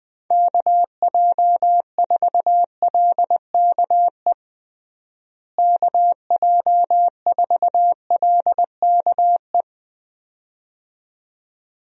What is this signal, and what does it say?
Morse code, heard as a keyed tone, about 20 wpm, 700 Hz. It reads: KJ4LKE KJ4LKE
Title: USA Call Signs - 6 Characters - Repeated 2x - 20wpm